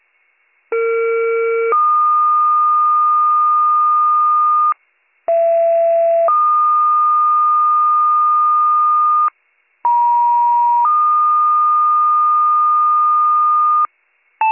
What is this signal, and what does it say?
Two-tone page, heard as a one-second tone, then a three-second tone, about 460 then 1220 Hz, about 670 then 1220 Hz, about 950 then 1220 Hz. It is heard from 0.7 to 13.9 s.